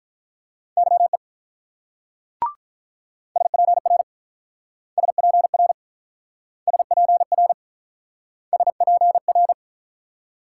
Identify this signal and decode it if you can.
Morse code, heard as a keyed tone, about 46 wpm, 700 Hz. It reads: YE SPR SPR SPR SPR